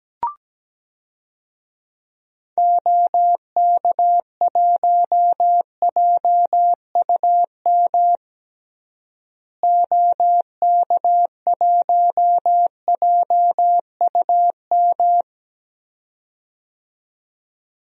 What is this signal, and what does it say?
Morse code, heard as a keyed tone, about 17 wpm, 700 Hz. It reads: OK1JUM OK1JUM
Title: DXCC-100 Call Signs - 6 Characters - Repeated 2x - 17wpm